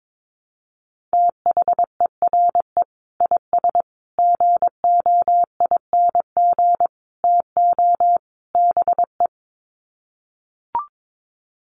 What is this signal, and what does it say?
Morse code, heard as a keyed tone, about 22 wpm, 700 Hz. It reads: THERE IS GOING TO BE
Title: Sets of 5 Words Encore 22wpm